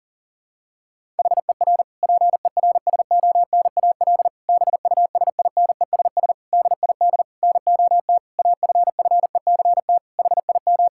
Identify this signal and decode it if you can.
Morse code, heard as a keyed tone, about 37 wpm, 700 Hz. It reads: HER PERSONAL BUSINESS DID NOT AFFECT HIM
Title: Sentences from Top 1000 Words - Review Part 6 - 40wpm